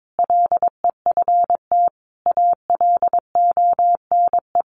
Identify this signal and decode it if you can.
Morse code, heard as a keyed tone, about 22 wpm, 700 Hz. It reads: LEFT ALONE
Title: Sentences from Top 1000 Words 22wpm